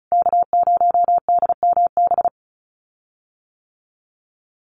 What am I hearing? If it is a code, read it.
Morse code, heard as a keyed tone, about 34 wpm, 700 Hz. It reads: K0DMB
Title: USA Call Signs Repeated 2x 35wpm